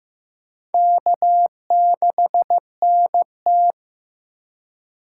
Morse code, heard as a keyed tone, about 15 wpm, 700 Hz. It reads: K6NT
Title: USA Call Signs - 4 Characters - 4x Character Spacing - 15wpm